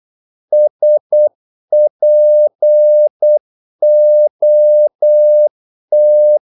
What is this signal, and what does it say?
Morse code, heard as a keyed tone, about 8 wpm, 600 Hz. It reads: SPOT
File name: Korte ord DK 8wpm